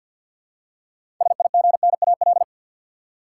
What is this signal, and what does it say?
Morse code, heard as a keyed tone, about 53 wpm, 700 Hz. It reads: SIGNAL
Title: Six Letter Words - 50wpm